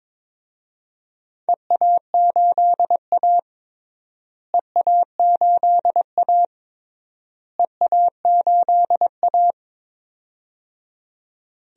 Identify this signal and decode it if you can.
Morse code, heard as a keyed tone, about 22 wpm, 700 Hz. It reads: EA8A EA8A EA8A
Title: DXCC-100 Call Signs - 4 Characters - Repeated 3x - 22wpm